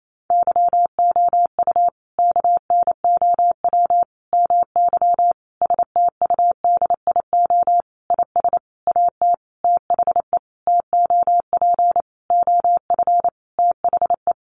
Morse code, heard as a keyed tone, about 28 wpm, 700 Hz. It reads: YOU KNOW MY STUDIO IS AT THE TOP OF THE
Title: Sentences from Top 1200 Words 28wpm